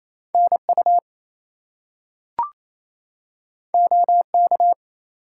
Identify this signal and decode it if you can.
Morse code, heard as a keyed tone, about 28 wpm, 700 Hz. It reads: NU OK
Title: US States and Canadian Provinces 28wpm